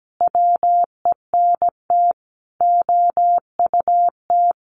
Morse code, heard as a keyed tone, about 17 wpm, 700 Hz. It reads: WENT OUT
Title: Sentences from Top 600 Words 17wpm